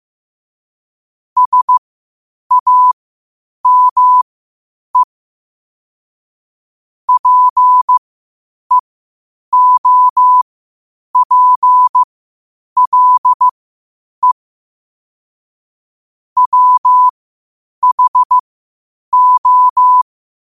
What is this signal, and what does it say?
Morse code, heard as a keyed tone, about 15 wpm, spaced as for 8 wpm, 1000 Hz. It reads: SAME PEOPLE WHO
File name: qotdfiles / 8wpm